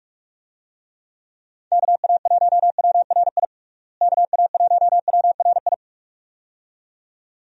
Morse code, heard as a keyed tone, about 43 wpm, 700 Hz. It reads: KA1WRI KA1WRI
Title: USA Call Signs - 6 Characters - Repeated 2x - 45wpm